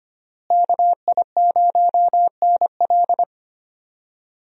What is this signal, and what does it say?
Morse code, heard as a keyed tone, about 25 wpm, 700 Hz. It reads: KI0NL
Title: USA Call Signs - 5 Characters - Repeated 3x - 25wpm